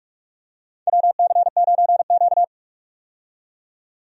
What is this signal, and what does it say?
Morse code, heard as a keyed tone, about 46 wpm, 700 Hz. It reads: WC9Q
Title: USA Call Signs - 4 Characters - 45wpm